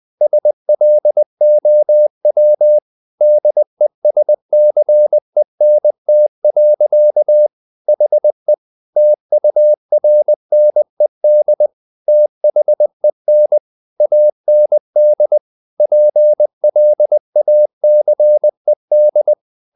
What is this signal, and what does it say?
Morse code, heard as a keyed tone, about 20 wpm, 600 Hz. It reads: SLOW DESCENT. HE TURNED THEN AND PLACED
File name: the old man and the sea part 2 20wpm